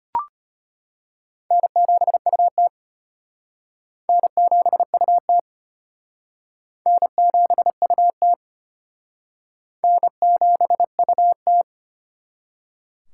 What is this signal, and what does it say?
Morse code, heard as a keyed tone, about 33 wpm, 700 Hz. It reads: N7UT N7UT N7UT N7UT